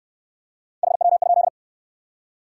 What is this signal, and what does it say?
Morse code, heard as a keyed tone, about 58 wpm, 700 Hz. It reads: HW?